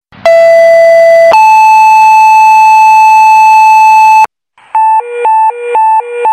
Two-tone page, heard as a one-second tone, then a three-second tone, about 670 then 870 Hz. It is heard from 0.3 to 4.3 s.
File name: go-download / 2aY